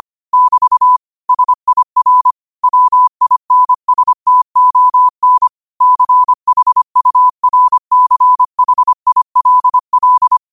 Morse code, heard as a keyed tone, about 25 wpm, 1000 Hz. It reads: <BT> SIR WINSTON CHURCHILL